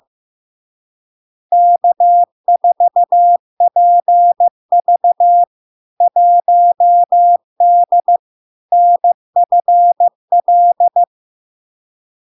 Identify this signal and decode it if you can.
Morse code, heard as a keyed tone, about 15 wpm, 700 Hz. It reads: K4PV 1D NFL
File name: LARK-FD-2022-Log-in-CW-15wpm-700-Hz